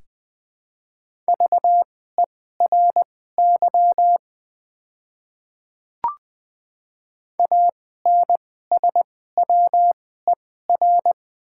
Morse code, heard as a keyed tone, about 20 wpm, spaced as for 15 wpm, 700 Hz. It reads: VERY ANSWER
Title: 2x Character Spacing - Top 300 Words - 20wpm